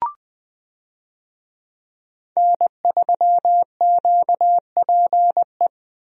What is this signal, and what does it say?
Morse code, heard as a keyed tone, about 20 wpm, 700 Hz. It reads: N3QPE
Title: USA Call Signs - 5 Characters - Repeated 2x - 20wpm